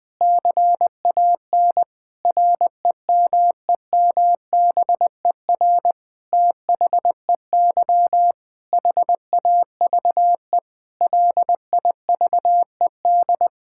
Morse code, heard as a keyed tone, about 21 wpm, 700 Hz. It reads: CAN REMEMBER THEY HAVE LIVED
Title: Sentences with 7-Word Phrases - 20wpm